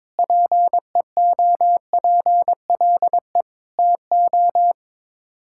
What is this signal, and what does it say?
Morse code, heard as a keyed tone, about 22 wpm, 700 Hz. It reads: PEOPLE TO
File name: practice / 22wpm / Sets of 7 Words - 22wpm